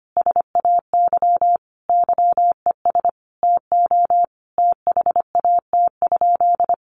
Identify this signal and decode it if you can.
Morse code, heard as a keyed tone, about 25 wpm, 700 Hz. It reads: SAY YES TO THAT?
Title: Sentences from Top 1100 Words 25wpm